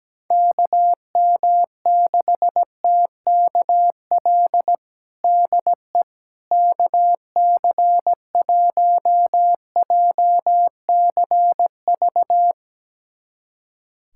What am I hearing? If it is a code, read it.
Morse code, heard as a keyed tone, about 17 wpm, 700 Hz. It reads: KM6TKL DE KC1JCV
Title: USA Callsign DE Callsign 17wpm